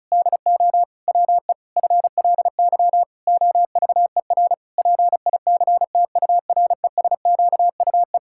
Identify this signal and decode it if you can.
Morse code, heard as a keyed tone, about 36 wpm, 700 Hz. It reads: DO WE FLY OVER PICTURESQUE
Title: Pangram Challenge v2 - 35wpm